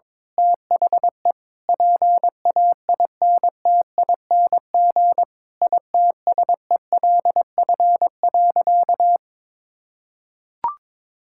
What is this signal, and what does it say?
Morse code, heard as a keyed tone, about 22 wpm, 700 Hz. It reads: THE PAINTING ITSELF.
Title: Sentences from Top 1000 Words - Review Part 6 - 22wpm